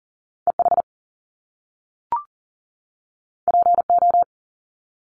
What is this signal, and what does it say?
Morse code, heard as a keyed tone, about 40 wpm, 700 Hz. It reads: EH PO
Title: Top Two Letter Combinations in Words 40wpm